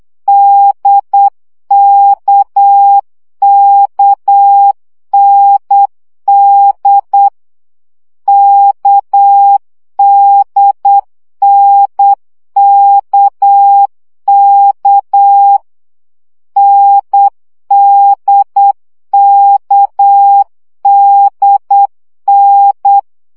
Morse code, heard as a keyed tone, about 8 wpm, 800 Hz. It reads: DKKND KDNKK NDKDN
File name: CW gyakorlas n d k 7 baud